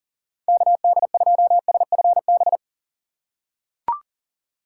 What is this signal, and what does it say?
Morse code, heard as a keyed tone, about 40 wpm, 700 Hz. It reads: KD2SFB
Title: USA Call Signs - 3x Character Spacing - 40wpm